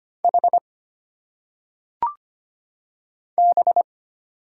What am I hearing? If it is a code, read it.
Morse code, heard as a keyed tone, about 25 wpm, 700 Hz. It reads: H B
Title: Single Letters - Phonetic Pronunciation 25wpm